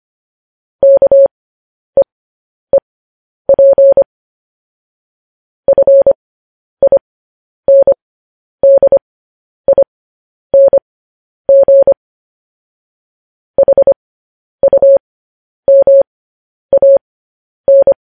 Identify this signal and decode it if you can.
Morse code, heard as a keyed tone, about 25 wpm, spaced as for 10 wpm, 550 Hz. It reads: KEEP FINDING HUMAN
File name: podcast / News headlines-1x-25x10